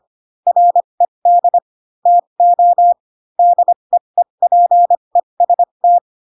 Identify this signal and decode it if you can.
Morse code, heard as a keyed tone, about 25 wpm, spaced as for 22 wpm, 700 Hz. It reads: RED TO DEEPEST